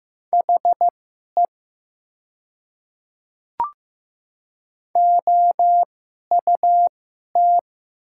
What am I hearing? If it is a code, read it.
Morse code, heard as a keyed tone, about 15 wpm, spaced as for 11 wpm, 700 Hz. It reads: HE OUT
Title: 2x Character Spacing - Top 100 Words - 15wpm